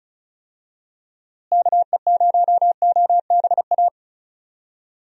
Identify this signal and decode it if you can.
Morse code, heard as a keyed tone, about 36 wpm, 700 Hz. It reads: KE0OBA
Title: USA Call Signs - 6 Characters - Repeated 2x - 35wpm